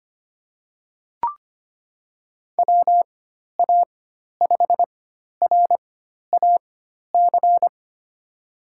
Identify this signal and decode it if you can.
Morse code, heard as a keyed tone, about 25 wpm, spaced as for 12 wpm, 700 Hz. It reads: WA5RAC
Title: USA Call Signs - 4x Character Spacing - 25wpm